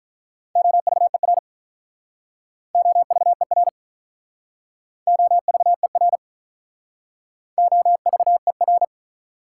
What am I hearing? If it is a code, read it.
Morse code, heard as a keyed tone, about 44 wpm, 690 Hz. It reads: OVER OVER OVER OVER